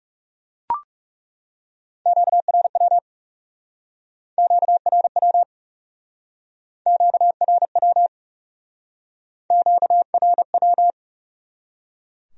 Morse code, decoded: QRW QRW QRW QRW

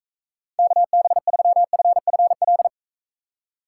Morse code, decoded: KB3FFL

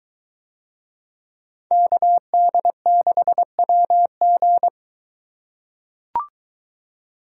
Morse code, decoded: KD6WG

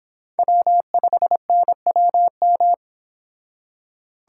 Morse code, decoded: W5NWM

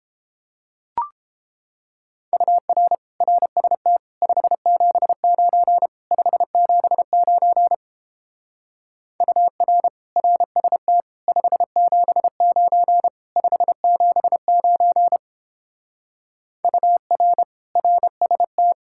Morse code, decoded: UR RST 579 579 UR RST 579 579 UR RST

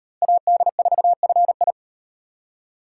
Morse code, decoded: AD4FI

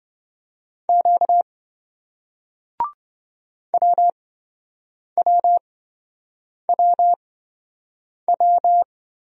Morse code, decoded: Q WWWW

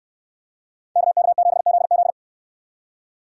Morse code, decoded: KK7ZZ

